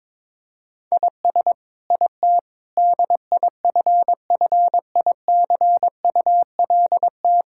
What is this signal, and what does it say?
Morse code, heard as a keyed tone, about 22 wpm, 700 Hz. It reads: IS IT DIFFICULT
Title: Sentences from Top 1000 Words - Review Part 6 - 22wpm